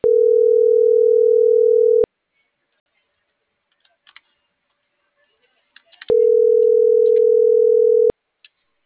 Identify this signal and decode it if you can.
Ringback tone (North America), 440 + 480 Hz.